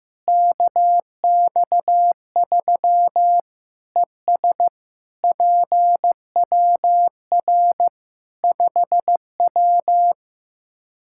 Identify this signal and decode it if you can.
Morse code, heard as a keyed tone, about 15 wpm, 700 Hz. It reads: KX3 ES PWR 5W
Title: Practice QSO - Exchange 2 - Info - 15wpm